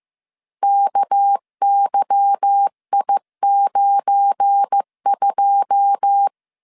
Morse code, heard as a keyed tone, about 15 wpm, 800 Hz. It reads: KYI92
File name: CW / DET / 1 9 1500C S A 14wpm 3222